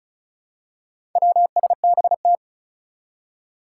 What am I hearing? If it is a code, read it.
Morse code, heard as a keyed tone, about 36 wpm, 700 Hz. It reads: WSBT
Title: Four Challenging Letter-Number Combinations 35wpm